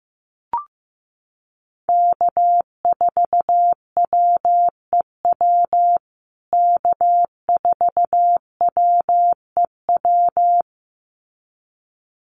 Morse code, heard as a keyed tone, about 15 wpm, 700 Hz. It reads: K4WEW K4WEW
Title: USA Call Signs Repeated 2x 15wpm